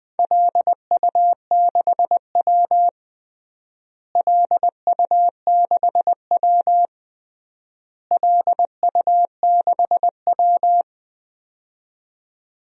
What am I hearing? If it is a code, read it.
Morse code, heard as a keyed tone, about 20 wpm, 700 Hz. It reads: LU6W LU6W LU6W
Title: DXCC-100 Call Signs - 4 Characters - Repeated 3x - 20wpm